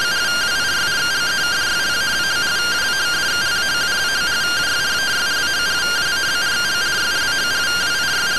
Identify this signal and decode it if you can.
Two-tone teleprinter signal, about 170 Hz shift.